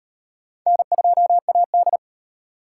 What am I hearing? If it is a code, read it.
Morse code, heard as a keyed tone, about 34 wpm, 700 Hz. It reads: N2AD